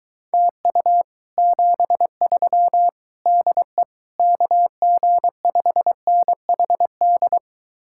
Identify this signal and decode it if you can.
Morse code, heard as a keyed tone, about 23 wpm, 700 Hz. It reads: TU 73 DE KG5NHD